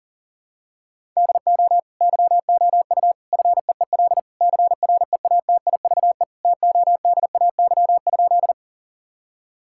Morse code, heard as a keyed tone, about 40 wpm, 700 Hz. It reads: DO YOU FEEL CREATIVE TODAY?
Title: Sentences from Top 1000 Words - Review Part 5 - 40wpm